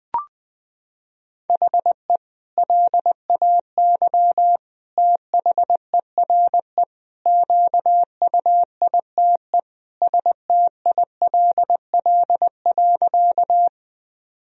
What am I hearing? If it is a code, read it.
Morse code, heard as a keyed tone, about 20 wpm, 700 Hz. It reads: HE LAY THERE QUITE STILL.
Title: Sentences from Top 1000 Words - Review Part 2 - 20wpm